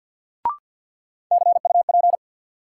Morse code, heard as a keyed tone, about 52 wpm, 700 Hz. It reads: CUP
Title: Sentences from Top 1600 Words - New Word Drill Alt - 50wpm